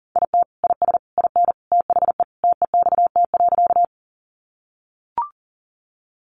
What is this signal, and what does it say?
Morse code, heard as a keyed tone, about 40 wpm, 700 Hz. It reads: IT IS IN THE TEXT.
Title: Sentences from Top 1200 Words 40wpm